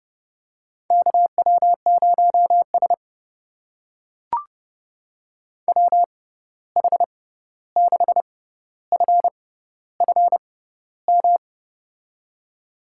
Morse code, heard as a keyed tone, about 31 wpm, spaced as for 14 wpm, 700 Hz. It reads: KW0S WH6FFM
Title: USA Call Signs - 6x Character Spacing - 30wpm